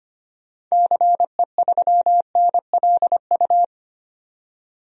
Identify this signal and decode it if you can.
Morse code, heard as a keyed tone, about 25 wpm, 700 Hz. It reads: CE3NLU
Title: DXCC-100 Call Signs - 6 Characters - 25wpm